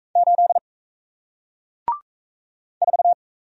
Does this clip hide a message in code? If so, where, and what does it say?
Yes: Morse code from 0.2 to 3.1 s, heard as a keyed tone, about 38 wpm, 700 Hz. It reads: 8 4